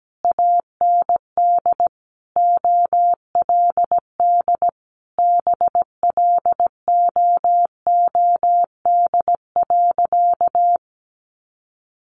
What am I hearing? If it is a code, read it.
Morse code, heard as a keyed tone, about 17 wpm, 700 Hz. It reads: AND OLD BLOOD.
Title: A Sound Of Thunder 17wpm